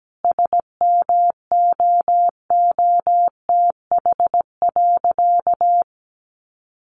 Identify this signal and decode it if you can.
Morse code, heard as a keyed tone, about 17 wpm, 700 Hz. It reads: SMOOTH.
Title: Sentences from Top 1400 Words 17wpm